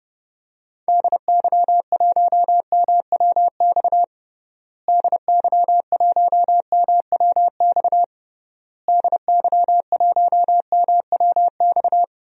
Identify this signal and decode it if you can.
Morse code, heard as a keyed tone, about 30 wpm, 700 Hz. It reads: DY1MWX DY1MWX DY1MWX
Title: DXCC-100 Call Signs - 6 Characters - Repeated 3x - 30wpm